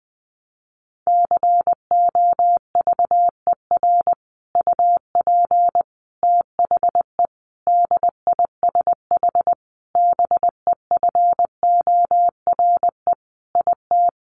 Morse code, heard as a keyed tone, about 20 wpm, 700 Hz. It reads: COVER UP THE DISH BEFORE IT
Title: Sentences from Top 1300 Words 20wpm